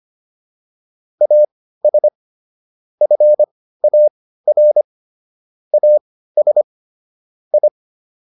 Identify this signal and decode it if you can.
Morse code, heard as a keyed tone, about 25 wpm, spaced as for 15 wpm, 600 Hz. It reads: AS FAR AS I
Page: intermediate-practice-files